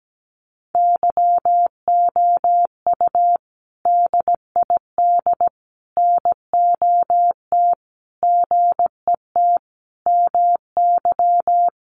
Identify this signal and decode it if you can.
Morse code, heard as a keyed tone, about 17 wpm, 700 Hz. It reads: YOU DID NOT GET MY
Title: Sentences from Top 600 Words 17wpm